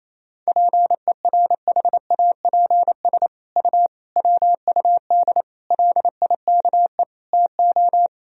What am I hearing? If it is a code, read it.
Morse code, heard as a keyed tone, about 28 wpm, 700 Hz. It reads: PERHAPS U WUD LIKE TO
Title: Sentences from Top 500 Words and Abbreviations 28wpm